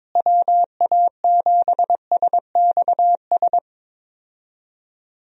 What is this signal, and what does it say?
Morse code, heard as a keyed tone, about 22 wpm, 700 Hz. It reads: WA7SXS
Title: USA Call Signs - 6 Characters - Repeated 3x - 22wpm